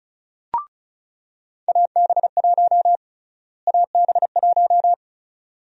Morse code, heard as a keyed tone, about 36 wpm, 700 Hz. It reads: AB1 AB1
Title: USA Call Sign Prefixes - Repeated 2x - 35wpm